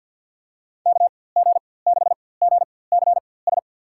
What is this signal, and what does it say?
Morse code, heard as a keyed tone, about 53 wpm, spaced as for 24 wpm, 700 Hz. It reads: KG6GCS
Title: USA Call Signs - 4x Character Spacing - 50wpm